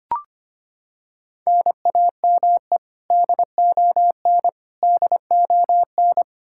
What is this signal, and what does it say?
Morse code, heard as a keyed tone, about 25 wpm, 700 Hz. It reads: NAME DON DON
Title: Practice QSO - Exchange 1 - Beginning - 25wpm